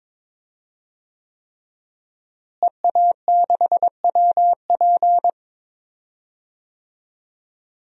Morse code, heard as a keyed tone, about 22 wpm, 700 Hz. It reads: EA6WP
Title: DXCC-100 Call Signs - 5 Characters - Repeated 3x - 22wpm